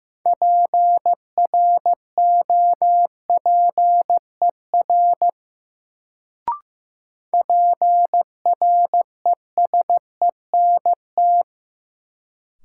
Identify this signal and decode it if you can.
Morse code, heard as a keyed tone, about 15 wpm, 700 Hz. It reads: PROPER PRESENT
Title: Sentences from Top 800 Words - New Word Drill - 15wpm